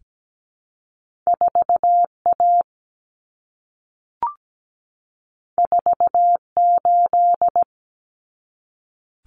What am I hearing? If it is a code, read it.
Morse code, heard as a keyed tone, about 17 wpm, 700 Hz. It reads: 4A 48
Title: Common Cut Numbers - 2-Digit 17wpm